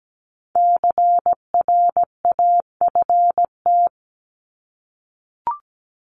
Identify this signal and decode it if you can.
Morse code, heard as a keyed tone, about 17 wpm, 700 Hz. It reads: CRAFT
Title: Top 751-1000 Words 17wpm